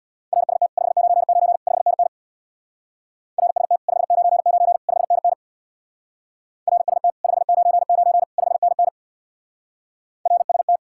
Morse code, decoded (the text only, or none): RST 599 5NN RST 599 5NN RST 599 5NN RST